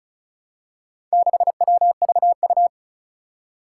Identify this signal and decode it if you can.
Morse code, heard as a keyed tone, about 36 wpm, 700 Hz. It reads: 6WVU
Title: Four Challenging Letter-Number Combinations 35wpm